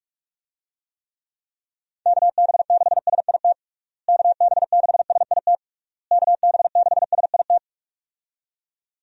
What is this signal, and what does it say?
Morse code, heard as a keyed tone, about 46 wpm, 700 Hz. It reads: KB6SIT KB6SIT KB6SIT
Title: USA Call Signs - 6 Characters - Repeated 3x - 45wpm